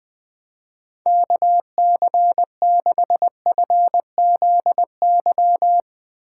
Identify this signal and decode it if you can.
Morse code, heard as a keyed tone, about 20 wpm, 700 Hz. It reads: KC6FZY
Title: Speed Racing - USA Call Signs 20wpm